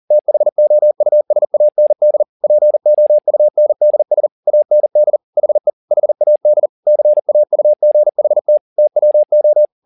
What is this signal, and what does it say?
Morse code, heard as a keyed tone, about 40 wpm, 600 Hz. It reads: THOUSAND POUNDS AND HE HAD CAUGHT TWO